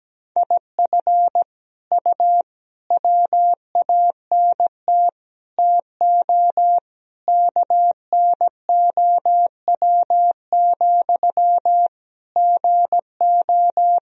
Morse code, heard as a keyed tone, about 17 wpm, 700 Hz. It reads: IF U WANT TO KNOW, GO
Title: Sentences from Top 200 Words and Abbreviations 17wpm